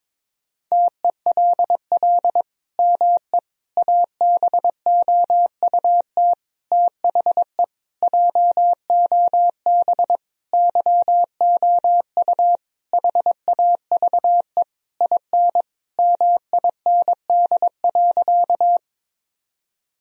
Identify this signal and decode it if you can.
Morse code, heard as a keyed tone, about 22 wpm, 700 Hz. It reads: TELL ME ABOUT THE JOB YOU HAVE IN MIND.
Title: Sentences from Top 1000 Words - Review Part 3 - 22wpm